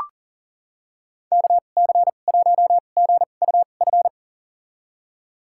Morse code, heard as a keyed tone, about 40 wpm, spaced as for 29 wpm, 690 Hz. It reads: KC1GUF
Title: USA Call Signs - 2x Character Spacing - 40wpm